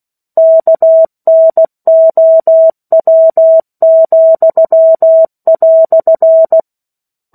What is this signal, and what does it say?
Morse code, heard as a keyed tone, about 16 wpm, 650 Hz. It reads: KNOW,"
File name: 03-Chapter-03 16 WPM